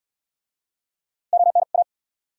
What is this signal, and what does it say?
Morse code, heard as a keyed tone, about 73 wpm, 700 Hz. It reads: ON A